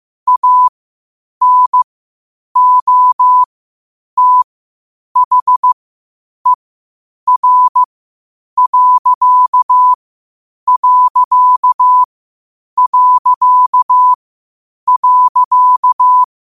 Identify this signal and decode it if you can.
Morse code, heard as a keyed tone, about 15 wpm, spaced as for 8 wpm, 1000 Hz. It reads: ANOTHER....
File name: qotdfiles / 8wpm